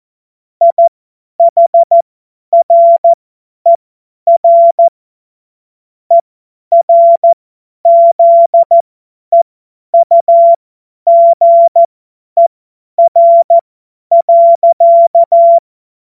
Morse code, heard as a keyed tone, about 14 wpm, spaced as for 10 wpm, 680 Hz. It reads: IHRER ERZEUGER.
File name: Morsekurs / Witze / Witz 04. Mai 2021, 50 F70